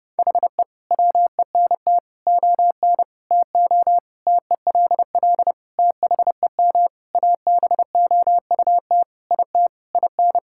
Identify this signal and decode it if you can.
Morse code, heard as a keyed tone, about 30 wpm, 700 Hz. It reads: HE WENT ON TO TELL THEM ABOUT IT IN
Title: Sentences from Top 1000 Words - Review Part 6 - 30wpm